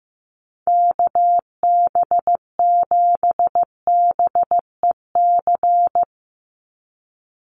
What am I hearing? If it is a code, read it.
Morse code, heard as a keyed tone, about 15 wpm, 700 Hz. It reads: KB7BEC
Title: USA Call Signs 15wpm